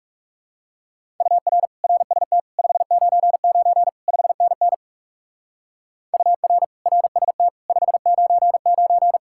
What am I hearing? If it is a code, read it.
Morse code, heard as a keyed tone, about 37 wpm, 700 Hz. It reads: UR RST 599 5NN UR RST 599
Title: Speed-Racing Common QSO Phrases Encore - 30wpm